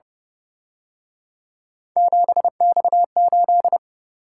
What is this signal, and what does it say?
Morse code, heard as a keyed tone, about 31 wpm, 700 Hz. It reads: 7X8
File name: Speed-Racing - Three Letter-Number Combinations - 25wpm